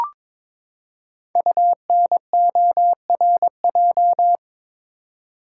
Morse code, heard as a keyed tone, about 22 wpm, 700 Hz. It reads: UNORJ
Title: Code Groups - 5 Letters - 22wpm